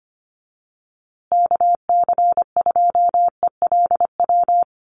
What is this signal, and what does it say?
Morse code, heard as a keyed tone, about 25 wpm, 700 Hz. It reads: KC2ELW
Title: USA Call Signs Repeated 2x 25wpm